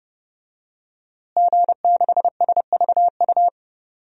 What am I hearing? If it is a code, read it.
Morse code, heard as a keyed tone, about 30 wpm, 700 Hz. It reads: G6SVU
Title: DXCC-100 Call Signs - 5 Characters - Repeated 3x - 30wpm